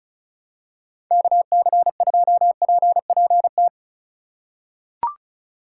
Morse code, heard as a keyed tone, about 36 wpm, 700 Hz. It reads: KC2PPT
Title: USA Call Signs - 6 Characters - 3x Character Spacing - 35wpm